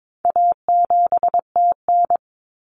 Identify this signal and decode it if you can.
Morse code, heard as a keyed tone, about 22 wpm, 700 Hz. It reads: A7TN
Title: Common Cut Numbers - 4-Digit 22wpm